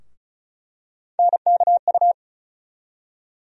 Morse code, heard as a keyed tone, about 36 wpm, 700 Hz. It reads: NKU
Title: USA Call Sign Postfixes - 3 Letters 35wpm